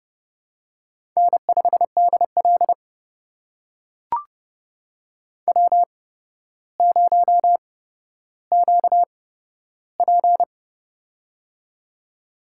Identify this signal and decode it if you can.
Morse code, heard as a keyed tone, about 32 wpm, spaced as for 13 wpm, 700 Hz. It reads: N5DL W0QP
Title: USA Call Signs - 4 Characters - 8x Character Spacing - 30wpm